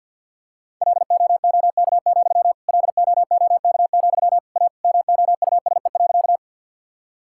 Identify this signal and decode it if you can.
Morse code, heard as a keyed tone, about 50 wpm, 700 Hz. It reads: LOOK, LOOK, A MOUSE.
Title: Sentences from Top 1000 Words - Review Part 5 - 50wpm